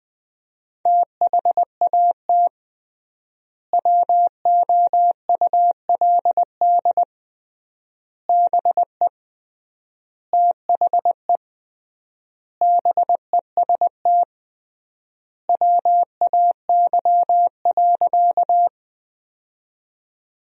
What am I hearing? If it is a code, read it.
Morse code, heard as a keyed tone, about 20 wpm, 700 Hz. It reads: THAT WOULD BE THE BEST WAY.
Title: 3x Word-Spacing - Sentences from Top 400 Words - 20wpm